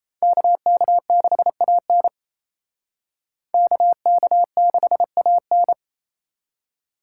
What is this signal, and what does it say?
Morse code, heard as a keyed tone, about 36 wpm, 700 Hz. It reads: KK6AN KK6AN